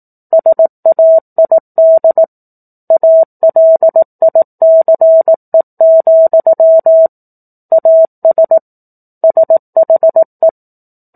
Morse code, heard as a keyed tone, about 18 wpm, 650 Hz. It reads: SAID ALICE, AS SHE
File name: 04-Chapter-04 18 WPM